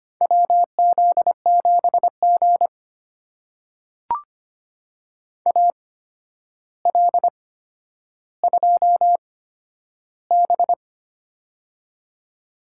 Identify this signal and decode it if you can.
Morse code, heard as a keyed tone, about 25 wpm, spaced as for 11 wpm, 700 Hz. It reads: WZ7G AL2B
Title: USA Call Signs - 4 Characters - 8x Character Spacing - 25wpm